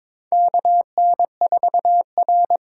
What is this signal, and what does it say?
Morse code, heard as a keyed tone, about 22 wpm, 700 Hz. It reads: KN4R
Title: USA Call Signs - 5 Characters - 8x Character Spacing - 22wpm